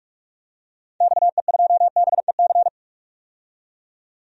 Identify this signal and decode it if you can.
Morse code, heard as a keyed tone, about 45 wpm, 700 Hz. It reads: XE2BEC
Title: DXCC-100 Call Signs - 6 Characters - 45wpm